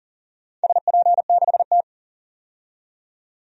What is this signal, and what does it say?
Morse code, heard as a keyed tone, about 40 wpm, 700 Hz. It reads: SP6T
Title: DXCC-100 Call Signs - 4 Characters - 40wpm